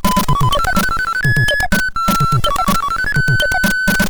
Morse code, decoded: none